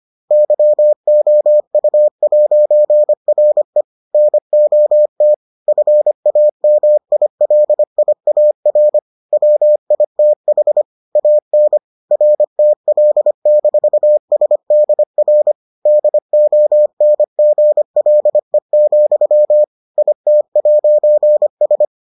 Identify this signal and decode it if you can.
Morse code, heard as a keyed tone, about 25 wpm, 600 Hz. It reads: YOU'RE NOT FAMILIAR WITH AN RTL-SDR DONGLE, IT'S